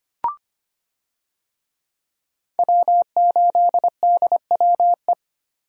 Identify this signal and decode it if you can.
Morse code, heard as a keyed tone, about 25 wpm, 700 Hz. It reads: W8DWE
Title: USA Call Signs - 5 Characters - 25wpm